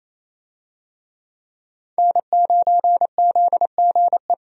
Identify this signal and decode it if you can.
Morse code, heard as a keyed tone, about 28 wpm, 700 Hz. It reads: N9ZGE
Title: USA Call Signs - 5 Characters - 28wpm